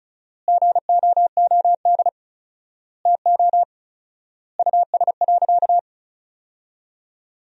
Morse code, heard as a keyed tone, about 35 wpm, 700 Hz. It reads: GOOD TO US.
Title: 4x Word-Spacing - Sentences from Top 200 Words - 35wpm